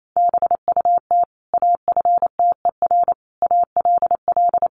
Morse code, heard as a keyed tone, about 28 wpm, 700 Hz. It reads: BUT AFTER ALL
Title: Sentences from Top 1500 Words 28wpm